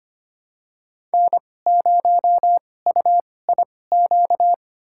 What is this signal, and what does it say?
Morse code, heard as a keyed tone, about 25 wpm, spaced as for 18 wpm, 700 Hz. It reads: N0UIQ
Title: USA Call Signs - 5 Characters - 2x Character Spacing - 25wpm